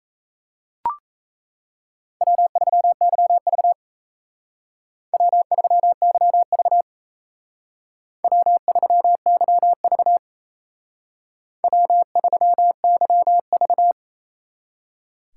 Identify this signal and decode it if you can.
Morse code, heard as a keyed tone, about 35 wpm, 700 Hz. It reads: W3YV W3YV W3YV W3YV